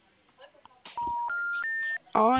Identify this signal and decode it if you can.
Special information tone, with speech.